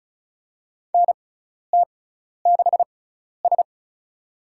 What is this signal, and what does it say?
Morse code, heard as a keyed tone, about 34 wpm, spaced as for 12 wpm, 700 Hz. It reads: NT6S
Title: USA Call Signs - 4 Characters - 6x Character Spacing - 35wpm